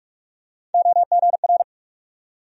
Morse code, heard as a keyed tone, about 42 wpm, 700 Hz. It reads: OGR
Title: USA Call Sign Postfixes - 3 Letters 45wpm